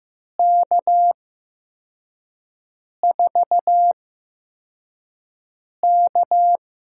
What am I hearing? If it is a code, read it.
Morse code, heard as a keyed tone, about 15 wpm, spaced as for 4 wpm, 700 Hz. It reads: K4K
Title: USA Call Signs - 4 Characters - 8x Character Spacing - 15wpm